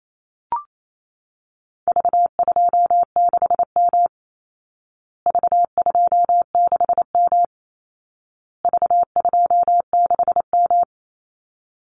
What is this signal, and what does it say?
Morse code, heard as a keyed tone, about 28 wpm, 700 Hz. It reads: V26M V26M V26M
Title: Call Signs Repeated 2x - 28wpm